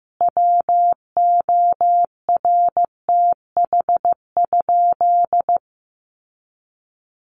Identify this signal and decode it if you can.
Morse code, heard as a keyed tone, about 15 wpm, 700 Hz. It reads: WORTH?
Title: Sentences from Top 800 Words 15wpm